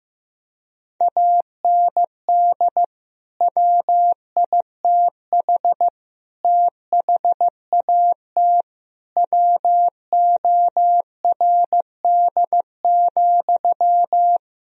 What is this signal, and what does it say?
Morse code, heard as a keyed tone, about 15 wpm, 700 Hz. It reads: AND WITH THAT WORD,
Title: Sentences from Top 1000 Words - Review Part 1 - 15wpm